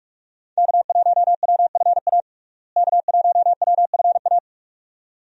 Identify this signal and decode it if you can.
Morse code, heard as a keyed tone, about 41 wpm, 700 Hz. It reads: K1WFA K1WFA
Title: USA Call Signs - 5 Characters - Repeated 2x - 45wpm